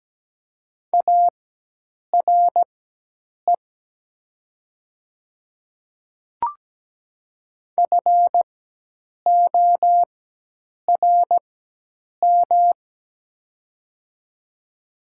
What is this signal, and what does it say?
Morse code, heard as a keyed tone, about 17 wpm, spaced as for 8 wpm, 700 Hz. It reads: ARE FORM
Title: 4x Character Spacing - Top 200 Words - 17wpm